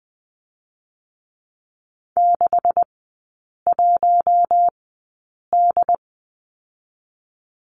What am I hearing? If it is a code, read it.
Morse code, heard as a keyed tone, about 19 wpm, spaced as for 8 wpm, 700 Hz. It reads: 61D
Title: 127 Lesson 21e - Number 6 - Three Characters 20wpm